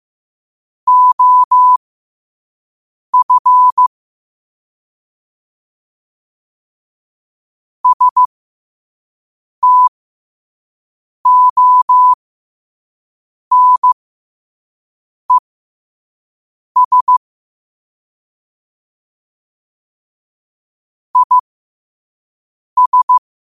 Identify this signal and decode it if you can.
Morse code, heard as a keyed tone, about 15 wpm, spaced as for 5 wpm, 1000 Hz. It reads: OF STONES IS
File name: qotdfiles / 5wpm